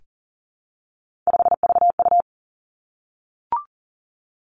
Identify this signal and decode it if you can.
Morse code, heard as a keyed tone, about 38 wpm, 700 Hz. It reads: 5VU
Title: Three Challenging Letter-Number Combinations 40wpm